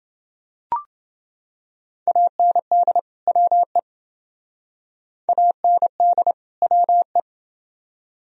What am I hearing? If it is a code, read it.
Morse code, heard as a keyed tone, about 28 wpm, 700 Hz. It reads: AND WE AND WE